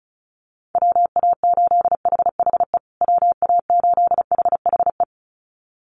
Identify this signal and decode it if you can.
Morse code, heard as a keyed tone, about 34 wpm, 700 Hz. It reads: WA8HHE WA8HHE
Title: USA Call Signs Repeated 2x 35wpm